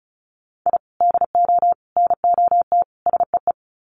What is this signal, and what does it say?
Morse code, heard as a keyed tone, about 34 wpm, 700 Hz. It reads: I DO NOT SEE
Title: Sets of 4 Words 35wpm